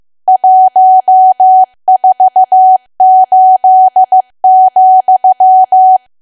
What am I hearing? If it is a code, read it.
Morse code, heard as a keyed tone, about 15 wpm, 750 Hz. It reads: 148,
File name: cwa / W1AW-Level2 / 15WPM